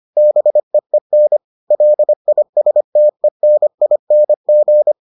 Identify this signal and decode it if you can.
Morse code, heard as a keyed tone, about 25 wpm, 600 Hz. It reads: BEEN LISTENING